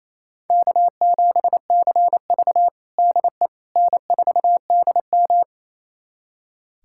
Morse code, heard as a keyed tone, about 28 wpm, 700 Hz. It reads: K7CV DE N4DM
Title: USA Callsign DE Callsign - 4 Characters - 28wpm